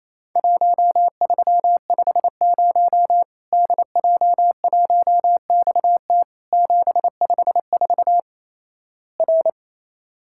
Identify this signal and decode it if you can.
Morse code, heard as a keyed tone, about 28 wpm, 700 Hz. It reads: 1350 DJ1XT 754 R
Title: Worked All Europe DX Contest - QTC Exchange - 28wpm